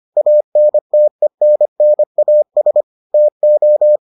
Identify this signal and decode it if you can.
Morse code, heard as a keyed tone, about 25 wpm, 600 Hz. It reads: ANTENNAS TO